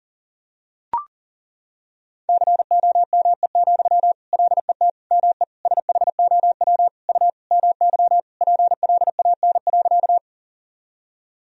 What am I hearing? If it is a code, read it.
Morse code, heard as a keyed tone, about 40 wpm, 700 Hz. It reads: COME, LET ME SHOW U MY PLAN.
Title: Sentences from Top 500 Words and Abbreviations 40wpm